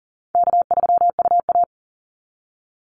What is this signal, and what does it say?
Morse code, heard as a keyed tone, about 40 wpm, 700 Hz. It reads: K3UA
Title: Call Signs Repeated 2x - 40wpm